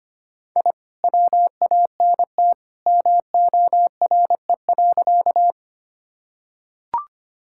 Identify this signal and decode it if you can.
Morse code, heard as a keyed tone, about 25 wpm, 700 Hz. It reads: I WANT MORE.
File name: practice / 25wpm / Sentences with 2-Word Phrases - 25wpm